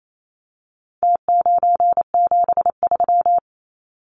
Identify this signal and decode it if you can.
Morse code, heard as a keyed tone, about 28 wpm, 700 Hz. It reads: T973
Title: Numbers - Contest Style - 4-Digit 28wpm